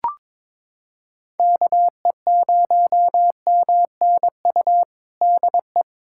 Morse code, heard as a keyed tone, about 22 wpm, 700 Hz. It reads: KE0MNU DE
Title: USA Callsign DE Callsign - 6 Characters - 22wpm